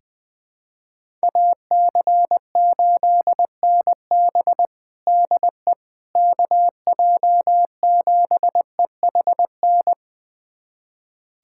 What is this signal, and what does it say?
Morse code, heard as a keyed tone, about 20 wpm, 700 Hz. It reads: AC8NB DE KJ7EHN
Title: USA Callsign DE Callsign 20wpm